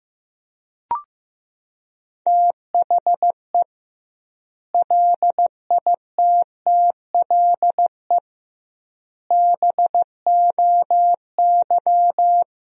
Morse code, heard as a keyed tone, about 15 wpm, 700 Hz. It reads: THE LITTLE BOY
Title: 2x Word-Spacing - Sentences from Top 500 Words - 15wpm